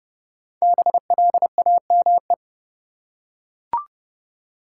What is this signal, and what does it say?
Morse code, heard as a keyed tone, about 30 wpm, 700 Hz. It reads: BLAME
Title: Five Letter Words - 30wpm